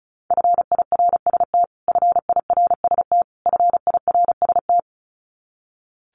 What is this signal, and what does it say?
Morse code, heard as a keyed tone, about 34 wpm, 700 Hz. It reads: FIRST FIRST FIRST
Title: Top 500 Words Repeated 35wpm